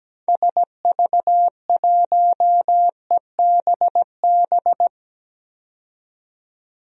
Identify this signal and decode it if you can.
Morse code, heard as a keyed tone, about 17 wpm, 700 Hz. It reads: SV1EBB
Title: DXCC-100 Call Signs - 6 Characters - 17wpm